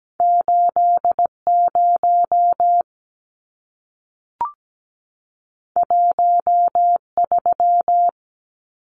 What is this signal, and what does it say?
Morse code, heard as a keyed tone, about 17 wpm, 700 Hz. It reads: 80 13
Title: Numbers - 2-Digit 17wpm